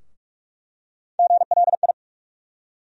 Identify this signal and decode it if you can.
Morse code, heard as a keyed tone, about 46 wpm, 700 Hz. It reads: GLI
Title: USA Call Sign Postfixes - 3 Letters 45wpm